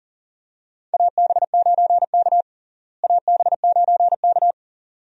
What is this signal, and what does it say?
Morse code, heard as a keyed tone, about 40 wpm, 700 Hz. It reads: AB9K AB9K
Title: USA Call Signs - 4 Characters - Repeated 2x - 40wpm